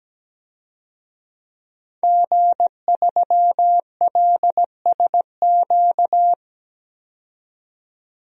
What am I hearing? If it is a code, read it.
Morse code, heard as a keyed tone, about 17 wpm, 700 Hz. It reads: G3LSQ
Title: DXCC-100 Call Signs - 5 Characters - 17wpm